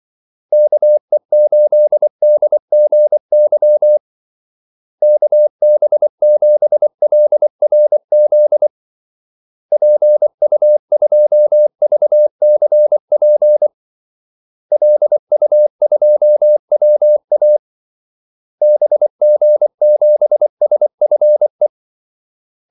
Morse code, heard as a keyed tone, about 24 wpm, 600 Hz. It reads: KE8DGY KB7LRZ PU2VCP LU2WA BG7SFE
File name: Callsigns-24wpm